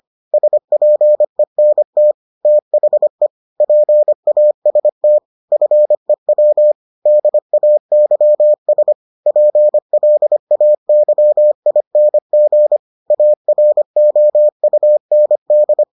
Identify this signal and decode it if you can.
Morse code, heard as a keyed tone, about 25 wpm, 600 Hz. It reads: SPENT THE PAST FEW DAYS PLAYING AROUND